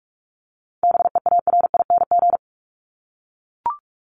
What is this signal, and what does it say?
Morse code, heard as a keyed tone, about 44 wpm, 700 Hz. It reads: BEARING
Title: ING Suffix - Top 100 Words with Base Words 45wpm